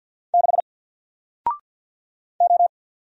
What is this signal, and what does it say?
Morse code, heard as a keyed tone, about 48 wpm, 700 Hz. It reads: 6 O
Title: Single Character (Letter:Number:Punctuation:Prosign) Alt 50wpm